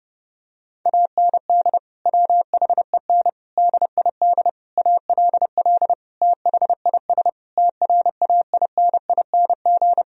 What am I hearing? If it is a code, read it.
Morse code, heard as a keyed tone, about 31 wpm, 700 Hz. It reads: AND WHEN DID ALL THIS TRAINING
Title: Sentences from Top 1000 Words - Review Part 4 - 30wpm